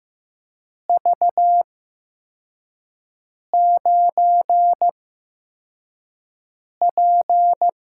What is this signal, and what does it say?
Morse code, heard as a keyed tone, about 15 wpm, spaced as for 4 wpm, 700 Hz. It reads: V9P
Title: USA Call Signs - 4 Characters - 8x Character Spacing - 15wpm